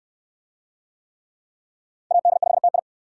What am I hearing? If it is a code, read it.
Morse code, heard as a keyed tone, about 66 wpm, 700 Hz. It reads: AD5TI